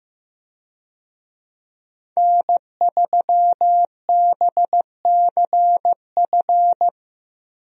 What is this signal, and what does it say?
Morse code, heard as a keyed tone, about 15 wpm, 700 Hz. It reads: N3BCF
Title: USA Call Signs - 5 Characters - Repeated 3x - 15wpm